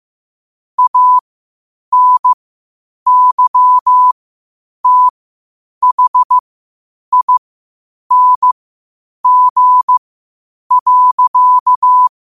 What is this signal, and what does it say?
Morse code, heard as a keyed tone, about 15 wpm, spaced as for 8 wpm, 1000 Hz. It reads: ANYTHING.
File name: qotdfiles / 8wpm